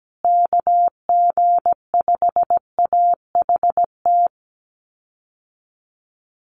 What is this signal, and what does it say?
Morse code, heard as a keyed tone, about 17 wpm, 700 Hz. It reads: KG5AHT
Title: USA Call Signs Repeated 3x 17wpm